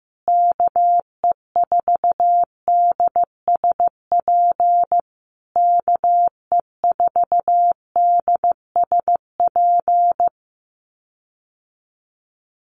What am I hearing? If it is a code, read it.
Morse code, heard as a keyed tone, about 15 wpm, 700 Hz. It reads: KE4DSP KE4DSP
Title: USA Call Signs Repeated 3x 15wpm